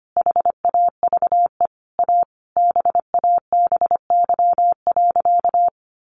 Morse code, heard as a keyed tone, about 25 wpm, 700 Hz. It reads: HAVE A BABY.
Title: Sentences from Top 1300 Words 25wpm